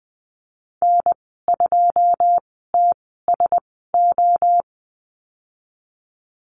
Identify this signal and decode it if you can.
Morse code, heard as a keyed tone, about 20 wpm, spaced as for 15 wpm, 700 Hz. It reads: N2TSO
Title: 082 Lesson 14g - Number 2 - Call Signs 20wpm